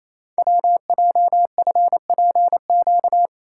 Morse code, heard as a keyed tone, about 28 wpm, 700 Hz. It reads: WJFPQ
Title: Code Groups - 5 Letters - 28wpm